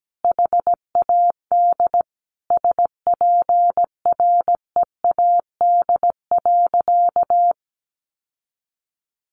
Morse code, heard as a keyed tone, about 17 wpm, 700 Hz. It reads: HAD SPREAD.
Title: Sentences from Top 1200 Words 17wpm